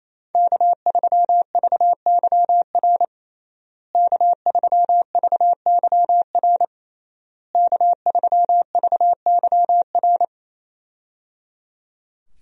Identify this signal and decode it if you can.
Morse code, heard as a keyed tone, about 27 wpm, 700 Hz. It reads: K3VYR K3VYR K3VYR
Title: USA Call Signs - 5 Characters - Repeated 3x - 28wpm